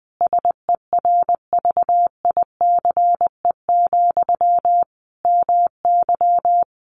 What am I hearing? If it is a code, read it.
Morse code, heard as a keyed tone, about 20 wpm, 700 Hz. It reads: SERVICE, MY
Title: Sentences from Top 1100 Words 20wpm